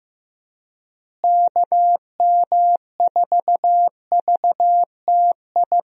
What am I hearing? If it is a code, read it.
Morse code, heard as a keyed tone, about 15 wpm, 700 Hz. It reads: KM4VTI
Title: USA Call Signs - 6 Characters - Repeated 2x - 15wpm